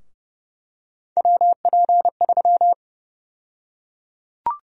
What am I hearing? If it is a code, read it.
Morse code, heard as a keyed tone, about 30 wpm, 700 Hz. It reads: WP3
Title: USA Call Sign Prefixes - 30wpm